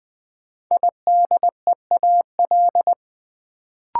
Morse code, heard as a keyed tone, about 20 wpm, 700 Hz. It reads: IDEAL
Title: Sentences from Top 1400 Words - New Word Drill - 20wpm